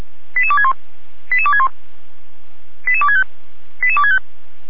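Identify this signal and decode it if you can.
Five-tone selective call (ZVEI), heard as quick runs of five tones, tones of about 70 ms. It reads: 88261 88261 88265 88265
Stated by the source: ZVEI five-tone call 88247 received on an SDR by Feuerwehr Orlamünde.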